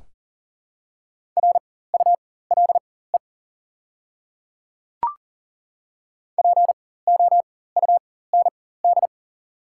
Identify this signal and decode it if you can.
Morse code, heard as a keyed tone, about 40 wpm, spaced as for 19 wpm, 690 Hz. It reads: RULE POUND
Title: 4x Character Spacing - Top 500 Words - 40wpm